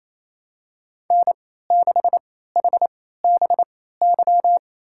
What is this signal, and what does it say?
Morse code, heard as a keyed tone, about 28 wpm, spaced as for 16 wpm, 700 Hz. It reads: N6HBY
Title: USA Call Signs - 3x Character Spacing - 28wpm